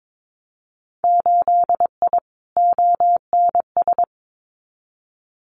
Morse code, heard as a keyed tone, about 22 wpm, 700 Hz. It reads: 8I ONS
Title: Winter Field Day Exchanges Encore 22wpm